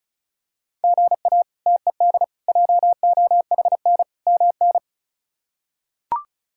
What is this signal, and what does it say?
Morse code, heard as a keyed tone, about 34 wpm, 700 Hz. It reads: GA TED JOHN MN
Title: K1USN SST QSO Practice - 35wpm